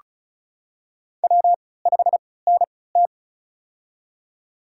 Morse code, heard as a keyed tone, about 36 wpm, spaced as for 20 wpm, 700 Hz. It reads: W5NT
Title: USA Call Signs - 4 Characters - 3x Character Spacing - 35wpm